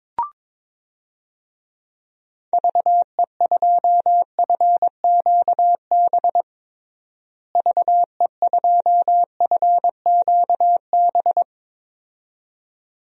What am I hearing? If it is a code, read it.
Morse code, heard as a keyed tone, about 22 wpm, 700 Hz. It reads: VE2FQB VE2FQB
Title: DXCC-100 Call Signs - 6 Characters - Repeated 2x - 22wpm